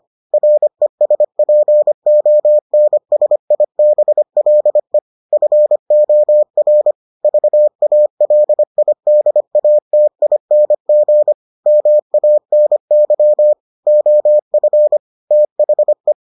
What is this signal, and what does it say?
Morse code, heard as a keyed tone, about 25 wpm, 600 Hz. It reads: RESPONSIBLE FOR VALIDATING MANY OF THE